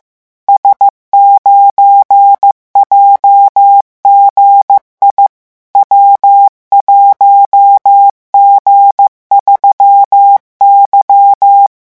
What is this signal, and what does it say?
Morse code, heard as a keyed tone, about 15 wpm, 800 Hz. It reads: S9JGI W1G3Y
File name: morse mp3 / revision12-vitesse 14 mots